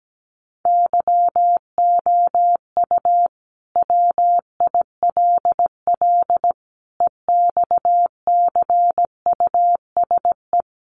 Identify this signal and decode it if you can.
Morse code, heard as a keyed tone, about 17 wpm, 700 Hz. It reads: YOU WILL EXCUSE
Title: Sentences from Top 700 Words 17wpm